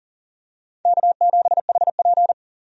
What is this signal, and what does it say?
Morse code, heard as a keyed tone, about 40 wpm, 700 Hz. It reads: K7HP
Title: USA Call Signs - 4 Characters - 40wpm